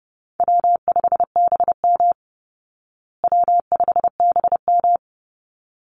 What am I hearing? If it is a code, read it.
Morse code, heard as a keyed tone, about 28 wpm, 700 Hz. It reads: W5BM W5BM
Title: Easy Call Signs Repeated 2x - 30wpm